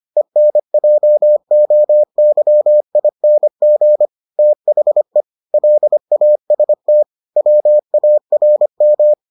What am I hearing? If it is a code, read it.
Morse code, heard as a keyed tone, about 25 wpm, 600 Hz. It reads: ENJOYING THE LAST WARM